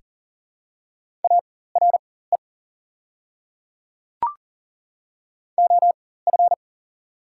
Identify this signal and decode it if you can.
Morse code, heard as a keyed tone, about 42 wpm, spaced as for 19 wpm, 690 Hz. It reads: ARE OF
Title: 4x Character Spacing - Top 200 Words - 40wpm